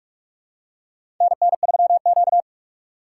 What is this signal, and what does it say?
Morse code, heard as a keyed tone, about 46 wpm, 700 Hz. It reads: NN3Q
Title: USA Call Signs - 4 Characters - 6x Character Spacing - 45wpm